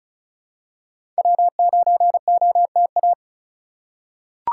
Morse code, heard as a keyed tone, about 36 wpm, 700 Hz. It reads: W9OTA
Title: USA Call Signs - 5 Characters - 4x Character Spacing - 35wpm